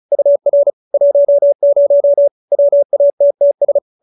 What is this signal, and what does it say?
Morse code, heard as a keyed tone, about 34 wpm, 580 Hz. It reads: UR 10 WATTS